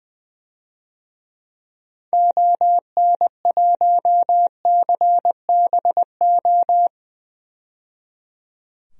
Morse code, heard as a keyed tone, about 21 wpm, 700 Hz. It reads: ON1CBO
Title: DXCC-100 Call Signs - 6 Characters - 20wpm